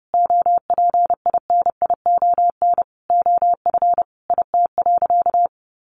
Morse code, heard as a keyed tone, about 30 wpm, 700 Hz. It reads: OPINION OF IT.
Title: Sentences from Top 1200 Words 30wpm